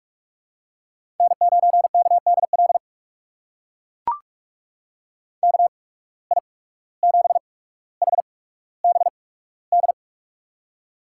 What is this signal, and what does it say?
Morse code, heard as a keyed tone, about 44 wpm, spaced as for 18 wpm, 700 Hz. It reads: N9KDL KI7HBD
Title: USA Call Signs - 8x Character Spacing - 45wpm